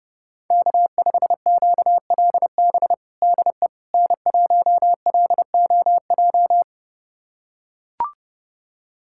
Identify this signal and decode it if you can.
Morse code, heard as a keyed tone, about 30 wpm, 700 Hz. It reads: K5QLB DE N1LOJ
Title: USA Callsign DE Callsign - 5 Characters - 30wpm